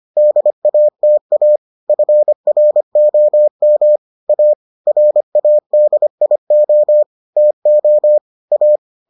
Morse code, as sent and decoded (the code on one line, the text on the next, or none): -.. .- - .- / ..-. .-. --- -- / .- / .-. .- -.. .. --- / - --- / .-
DATA FROM A RADIO TO A